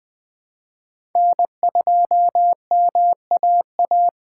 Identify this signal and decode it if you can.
Morse code, heard as a keyed tone, about 20 wpm, 700 Hz. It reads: N2MAA
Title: USA Call Signs - 5 Characters - 3x Character Spacing - 20wpm